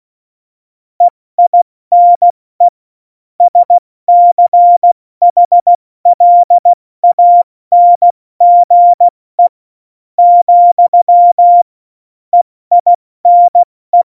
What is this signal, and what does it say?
Morse code, heard as a keyed tone, about 16 wpm, spaced as for 13 wpm, 700 Hz. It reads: EINE SCHLANGE, EINE
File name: Morsekurs / Witze / Witz 27 March 2018, 70 F80